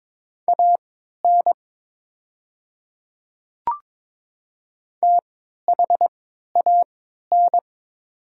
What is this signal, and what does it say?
Morse code, heard as a keyed tone, about 22 wpm, spaced as for 12 wpm, 700 Hz. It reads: AN THAN
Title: 3x Character Spacing - Top 200 Words - 22wpm